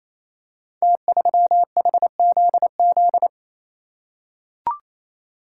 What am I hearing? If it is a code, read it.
Morse code, heard as a keyed tone, about 27 wpm, 700 Hz. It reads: T3HZZ
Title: Code Groups - 5 Letter:Numbers - 28wpm